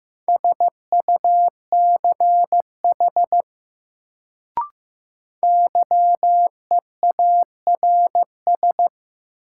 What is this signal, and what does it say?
Morse code, heard as a keyed tone, about 15 wpm, 700 Hz. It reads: SUCH YEARS
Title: Sentences from Top 200 Words - New Word Drill - 15wpm